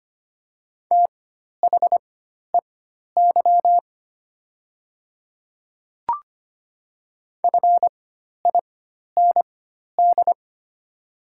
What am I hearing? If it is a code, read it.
Morse code, heard as a keyed tone, about 25 wpm, spaced as for 12 wpm, 700 Hz. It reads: THEY FIND
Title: 4x Character Spacing - Top 100 Words - 25wpm